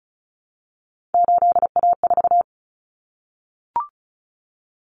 Morse code, heard as a keyed tone, about 34 wpm, 700 Hz. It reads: 8A4
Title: Common Cut Numbers - 3-Digit 35wpm